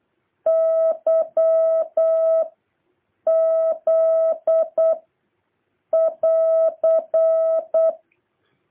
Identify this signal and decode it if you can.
Morse code, heard as a keyed tone, about 8 wpm, 650 Hz. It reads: Y Z <AR>
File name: jh8cbh / 2023cw